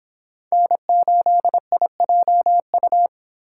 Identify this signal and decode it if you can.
Morse code, heard as a keyed tone, about 26 wpm, 700 Hz. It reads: N8IJU